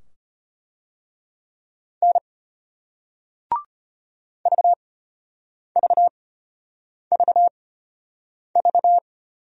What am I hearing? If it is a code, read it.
Morse code, heard as a keyed tone, about 34 wpm, spaced as for 8 wpm, 700 Hz. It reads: N VVVV